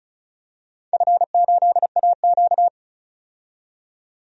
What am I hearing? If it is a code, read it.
Morse code, heard as a keyed tone, about 36 wpm, 700 Hz. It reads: F8AQ
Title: DXCC-100 Call Signs - 4 Characters - 35wpm